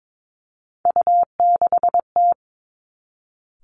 Morse code, heard as a keyed tone, about 22 wpm, 700 Hz. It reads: U6T
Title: Three Challenging Letter-Number Combinations 22wpm